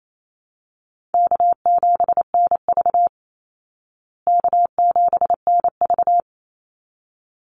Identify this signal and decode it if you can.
Morse code, heard as a keyed tone, about 28 wpm, 700 Hz. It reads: K7NV K7NV
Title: Easy Call Signs Repeated 2x - 28wpm